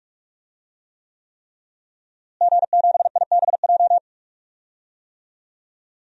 Morse code, heard as a keyed tone, about 46 wpm, 700 Hz. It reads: G7IBJ
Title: DXCC-100 Call Signs - 5 Characters - Repeated 3x - 45wpm